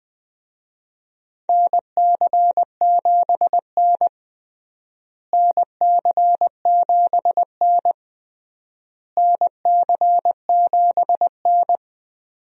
Morse code, heard as a keyed tone, about 21 wpm, 700 Hz. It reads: NC7N NC7N NC7N
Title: USA Call Signs - 4 Characters - Repeated 3x - 20wpm